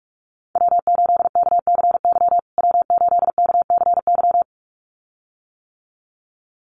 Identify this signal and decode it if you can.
Morse code, heard as a keyed tone, about 44 wpm, 700 Hz. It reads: W8KCY W8KCY
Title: USA Call Signs Repeated 2x 45wpm